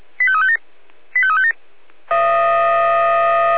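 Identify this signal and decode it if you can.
Five-tone selective call (ZVEI), heard as quick runs of five tones, tones of about 70 ms. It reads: 85357 85357
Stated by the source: ZVEI five-tone call received by SDR.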